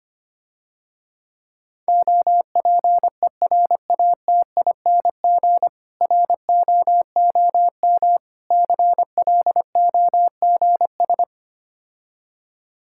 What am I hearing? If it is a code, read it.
Morse code, heard as a keyed tone, about 25 wpm, 700 Hz. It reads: OPERATING ROOM CLOGS
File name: practice / 25wpm / Difficult Words - Speed-Racing with Hints - 25wpm